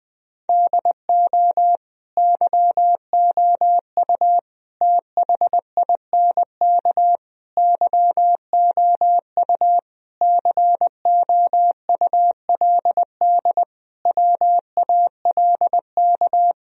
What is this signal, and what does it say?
Morse code, heard as a keyed tone, about 20 wpm, 700 Hz. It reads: DO YOU THINK YOU COULD WALK
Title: Sentences from Top 1000 Words - Review Part 1 - 20wpm